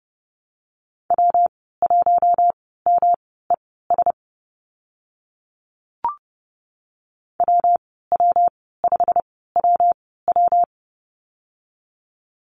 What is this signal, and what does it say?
Morse code, heard as a keyed tone, about 30 wpm, spaced as for 17 wpm, 700 Hz. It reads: W1MES WW5WW
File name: Lesson 19g - Letter w - Call Signs 30wpm